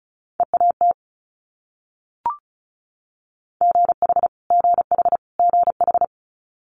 Morse code, heard as a keyed tone, about 34 wpm, 700 Hz. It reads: EAT GH GH GH
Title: Multi-letter Phonemes Repeated 3x 35wpm